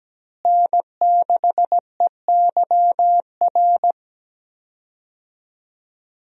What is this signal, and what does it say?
Morse code, heard as a keyed tone, about 17 wpm, 700 Hz. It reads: N6EYR
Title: USA Call Signs - 5 Characters - Repeated 3x - 17wpm